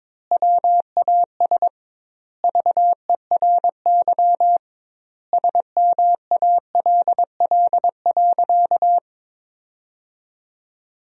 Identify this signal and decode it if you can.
Morse code, heard as a keyed tone, about 22 wpm, 700 Hz. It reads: WAS VERY SMALL.
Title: 2x Word-Spacing - Sentences from Top 400 Words - 22wpm